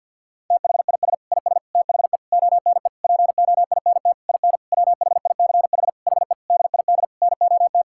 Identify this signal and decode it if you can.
Morse code, heard as a keyed tone, about 50 wpm, 700 Hz. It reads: THIS IS THE ONE POINT IN WHICH HE DID NOT
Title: Sentences from Top 1000 Words - Review Part 5 - 50wpm